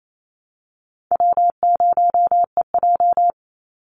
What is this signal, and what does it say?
Morse code, heard as a keyed tone, about 28 wpm, 700 Hz. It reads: W0EJ
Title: Easy Call Signs Repeated 3x - 28wpm